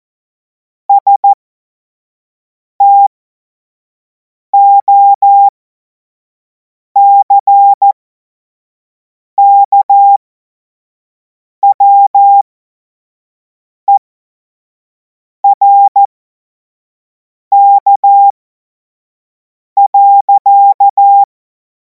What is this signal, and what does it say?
Morse code, heard as a keyed tone, about 14 wpm, spaced as for 5 wpm, 800 Hz. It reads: STOCKWERK.
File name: Morsekurs / Witze / Witz 03. Februar 2026 25 F70